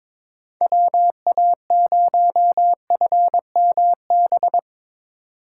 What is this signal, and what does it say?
Morse code, heard as a keyed tone, about 25 wpm, 700 Hz. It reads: WA0FMB
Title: Speed Racing - USA 2x3 Call Signs 22wpm